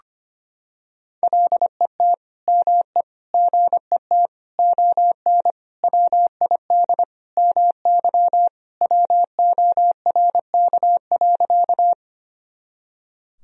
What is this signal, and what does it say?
Morse code, heard as a keyed tone, about 25 wpm, 700 Hz. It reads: LET ME GET ON WID MY WORK.
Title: Sentences from Top 300 Words and Abbreviations 25wpm